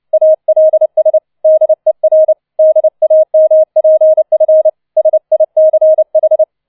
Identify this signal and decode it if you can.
Morse code, heard as a keyed tone, about 29 wpm, 630 Hz. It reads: ALS DER DAMPF SICH